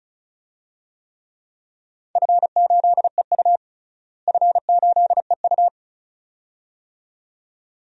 Morse code, heard as a keyed tone, about 34 wpm, 700 Hz. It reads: F8EU F8EU
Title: DXCC-100 Call Signs - 4 Characters - Repeated 2x - 35wpm